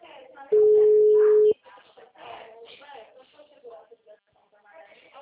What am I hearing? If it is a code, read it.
Ringback tone (Europe, 425 Hz), with speech.
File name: out- -2002- - - .79593